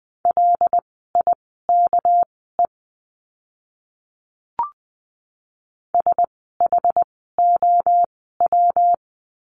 Morse code, heard as a keyed tone, about 20 wpm, spaced as for 14 wpm, 700 Hz. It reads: LIKE SHOW
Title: 206 Lesson 33f - Letter j - Words 20wpm